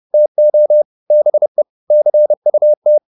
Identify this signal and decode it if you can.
Morse code, heard as a keyed tone, about 30 wpm, 600 Hz. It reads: TO BE CUT
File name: the old man and the sea / the old man and the sea 1 30wpm